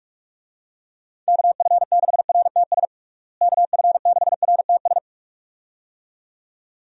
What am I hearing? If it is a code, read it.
Morse code, heard as a keyed tone, about 46 wpm, 700 Hz. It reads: KF6RTS KF6RTS
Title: USA Call Signs - 6 Characters - Repeated 2x - 45wpm